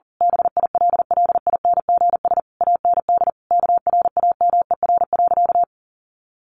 Morse code, heard as a keyed tone, about 40 wpm, 700 Hz. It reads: BILLINGS AND KRAMER.
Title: A Sound Of Thunder 40wpm